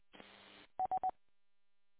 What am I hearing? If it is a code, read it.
There is no code.